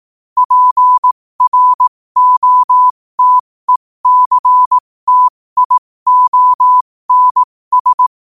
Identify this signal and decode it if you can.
Morse code, heard as a keyed tone, about 18 wpm, spaced as for 15 wpm, 1000 Hz. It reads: PROTECTIONS
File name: qotdfiles / 15wpm